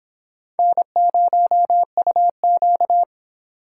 Morse code, heard as a keyed tone, about 25 wpm, 700 Hz. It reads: N0UQ